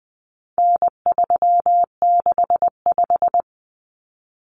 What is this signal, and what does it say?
Morse code, heard as a keyed tone, about 20 wpm, 700 Hz. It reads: N365
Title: Common Cut Numbers - 4-Digit 20wpm